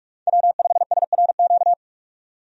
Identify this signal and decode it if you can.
Morse code, heard as a keyed tone, about 46 wpm, 700 Hz. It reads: W5SRQ
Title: USA Call Signs - 5 Characters - 45wpm